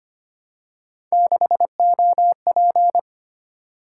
Morse code, heard as a keyed tone, about 25 wpm, 700 Hz. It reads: 6OP
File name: Three Letter-Number Combinations - 25wpm